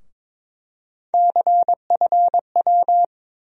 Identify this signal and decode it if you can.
Morse code, heard as a keyed tone, about 22 wpm, 700 Hz. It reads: CFW
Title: USA Call Sign Postfixes - 3 Letters 22wpm